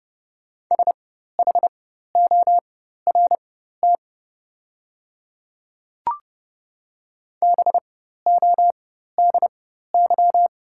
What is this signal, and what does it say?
Morse code, heard as a keyed tone, about 30 wpm, spaced as for 14 wpm, 700 Hz. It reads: SHORT BODY
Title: 4x Character Spacing - Top 500 Words - 30wpm